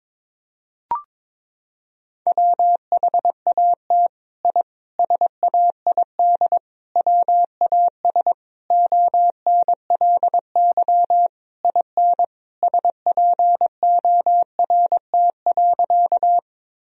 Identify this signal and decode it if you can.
Morse code, heard as a keyed tone, about 22 wpm, 700 Hz. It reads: WHAT I SAID WAS ONLY IN SPORT.
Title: Sentences from Top 1000 Words - Review Part 4 - 22wpm